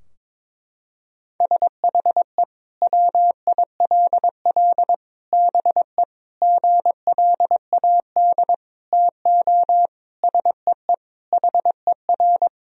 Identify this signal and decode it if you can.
Morse code, heard as a keyed tone, about 22 wpm, 700 Hz. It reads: SHE WILL BE GLAD TO SEE HER
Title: Sentences from Top 1000 Words - Review Part 2 - 22wpm